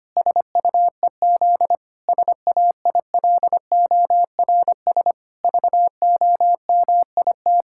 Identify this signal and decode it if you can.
Morse code, heard as a keyed tone, about 25 wpm, 700 Hz. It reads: SUEZ SAILORS VOMIT
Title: Pangram Challenge v2 - 25wpm